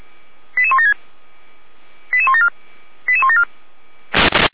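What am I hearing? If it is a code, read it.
Five-tone selective call (ZVEI), heard as quick runs of five tones, tones of about 70 ms. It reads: 88176 88174 88174
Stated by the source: ZVEI five-tone selective call received on an SDR, from the fire brigade (Feuerwehr) of Orlamünde.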